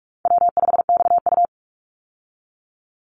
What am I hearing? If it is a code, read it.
Morse code, heard as a keyed tone, about 44 wpm, 700 Hz. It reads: W5XU
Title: Easy Call Signs Repeated 2x - 45wpm